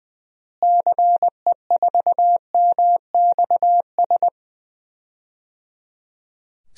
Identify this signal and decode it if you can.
Morse code, heard as a keyed tone, about 20 wpm, 700 Hz. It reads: CE4MXS
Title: DXCC-100 Call Signs - 6 Characters - Repeated 3x - 20wpm